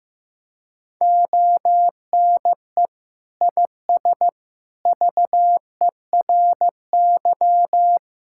Morse code, heard as a keyed tone, about 15 wpm, 700 Hz. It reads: ONE IS VERY
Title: Sentences from Top 1000 Words - Review Part 5 - 15wpm